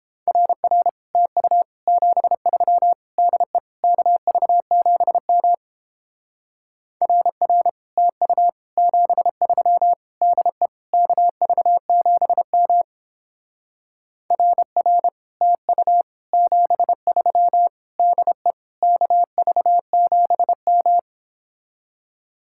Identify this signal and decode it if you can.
Morse code, heard as a keyed tone, about 31 wpm, 700 Hz. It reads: RR TU 73 DE KV7M RR TU 73 DE KV7M RR TU 73 DE KV7M